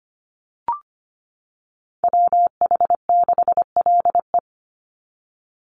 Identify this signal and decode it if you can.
Morse code, heard as a keyed tone, about 25 wpm, 700 Hz. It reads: WH6LE
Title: NA SOTA Chaser Callsigns 25wpm